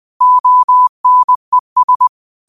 Morse code, heard as a keyed tone, about 20 wpm, 1000 Hz. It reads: ONES